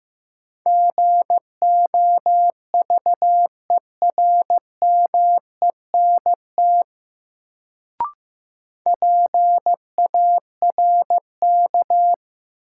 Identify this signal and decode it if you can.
Morse code, heard as a keyed tone, about 15 wpm, 700 Hz. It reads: GOVERMENT PARK
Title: Sentences from Top 1300 Words - New Word Drill - 15wpm